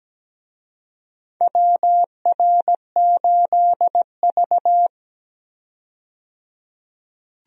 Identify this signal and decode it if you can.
Morse code, heard as a keyed tone, about 17 wpm, 700 Hz. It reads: WR8V
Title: USA Call Signs - 4 Characters - 17wpm